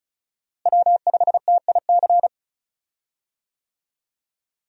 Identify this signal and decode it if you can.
Morse code, heard as a keyed tone, about 34 wpm, 700 Hz. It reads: W5TIC
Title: USA Call Signs - 5 Characters - Repeated 2x - 35wpm